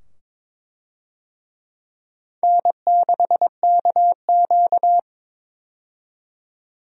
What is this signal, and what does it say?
Morse code, heard as a keyed tone, about 22 wpm, 700 Hz. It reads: N6KQ
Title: USA Call Signs - 4 Characters - 22wpm